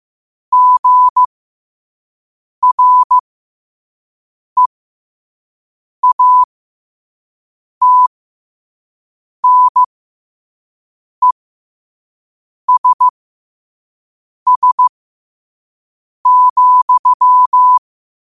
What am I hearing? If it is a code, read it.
Morse code, heard as a keyed tone, about 15 wpm, spaced as for 5 wpm, 1000 Hz. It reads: GREATNESS,
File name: qotdfiles / 5wpm